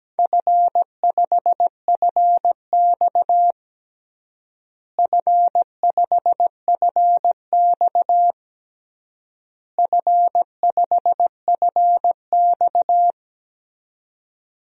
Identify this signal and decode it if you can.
Morse code, heard as a keyed tone, about 17 wpm, 700 Hz. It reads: F5FX F5FX F5FX
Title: DXCC-100 Call Signs - 4 Characters - Repeated 3x - 17wpm